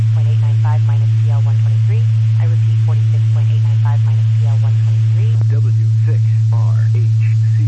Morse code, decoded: none

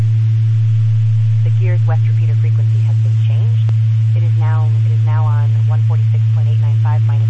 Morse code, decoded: none

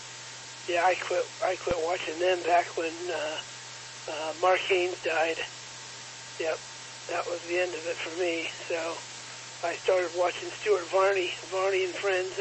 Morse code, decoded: none

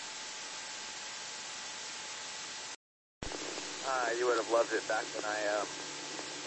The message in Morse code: none

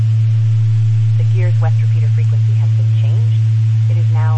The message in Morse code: none

